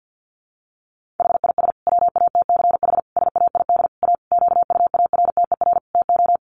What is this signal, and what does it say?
Morse code, heard as a keyed tone, about 50 wpm, 700 Hz. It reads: HIS WATCH SAID A QUARTER TO
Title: Sentences from Top 1300 Words 50wpm